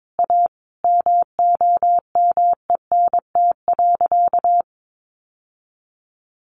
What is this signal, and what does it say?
Morse code, heard as a keyed tone, about 22 wpm, 700 Hz. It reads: A MOMENT.
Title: Sentences from Top 1000 Words 22wpm